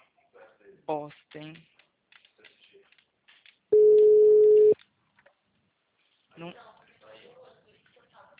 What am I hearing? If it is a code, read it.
Ringback tone (Europe, 425 Hz), with speech.